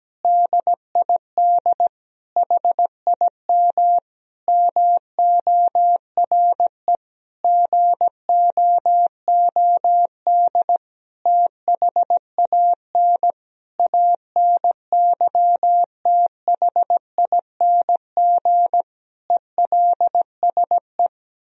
Morse code, heard as a keyed tone, about 17 wpm, 700 Hz. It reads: DID HIM MORE GOOD THAN ANYTHING ELSE
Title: Sentences from Top 1000 Words - Review Part 4 - 17wpm